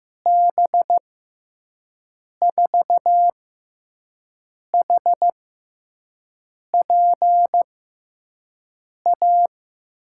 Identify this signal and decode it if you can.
Morse code, heard as a keyed tone, about 15 wpm, spaced as for 5 wpm, 700 Hz. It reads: B4HPA
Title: USA Call Signs - 6 Characters - 6x Character Spacing - 15wpm